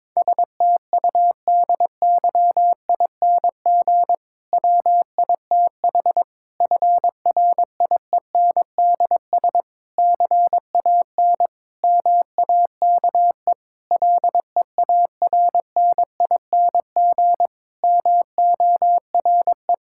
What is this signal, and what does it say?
Morse code, heard as a keyed tone, about 22 wpm, 700 Hz. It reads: STUDYING WITH FRIENDS CAN MAKE LEARNING MORE
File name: Difficult Sentences with Hints - 22wpm